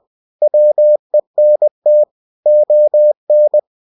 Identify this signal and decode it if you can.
Morse code, heard as a keyed tone, about 20 wpm, 600 Hz. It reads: WENT ON
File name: cw-books / winnie-the-pooh / winnie-20wpm-page-0021